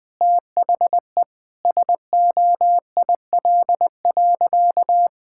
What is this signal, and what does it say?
Morse code, heard as a keyed tone, about 20 wpm, 700 Hz. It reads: THE SOIL.
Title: Sentences from Top 1000 Words - Review Part 5 - 20wpm